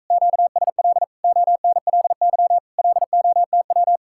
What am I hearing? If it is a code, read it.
Morse code, heard as a keyed tone, about 45 wpm, 700 Hz. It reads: QSL ONLY LOTW